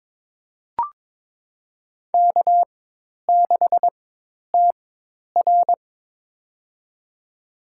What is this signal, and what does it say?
Morse code, heard as a keyed tone, about 22 wpm, spaced as for 10 wpm, 700 Hz. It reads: K6TR
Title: USA Call Signs - 4x Character Spacing - 22wpm